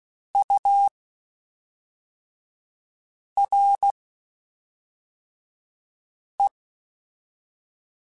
Morse code, heard as a keyed tone, about 16 wpm, 800 Hz. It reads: U R E
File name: Spaced Characters / Spaced Characters-mp3s / Weather5 5 sp @5